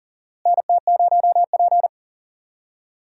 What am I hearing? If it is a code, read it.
Morse code, heard as a keyed tone, about 37 wpm, 700 Hz. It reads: NT0P
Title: USA Call Signs - 4 Characters - Repeated 3x - 40wpm